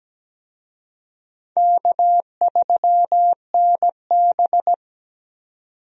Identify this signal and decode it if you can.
Morse code, heard as a keyed tone, about 17 wpm, 700 Hz. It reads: K3NB
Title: USA Call Signs - 4 Characters - 17wpm